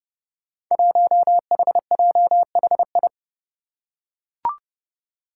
Morse code, heard as a keyed tone, about 30 wpm, 700 Hz. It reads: 1HJHI
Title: Five Challenging Letter-Number Combinations 30wpm